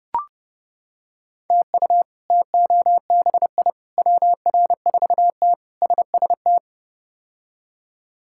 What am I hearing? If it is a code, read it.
Morse code, heard as a keyed tone, about 29 wpm, 700 Hz. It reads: TU TOBI WR4T SST
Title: K1USN SST QSO Practice - 30wpm